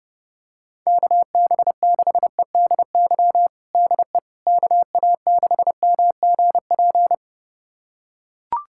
Morse code, heard as a keyed tone, about 30 wpm, 700 Hz. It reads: KB6EDY DE KA6MGP
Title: USA Callsign DE Callsign - 6 Characters - 30wpm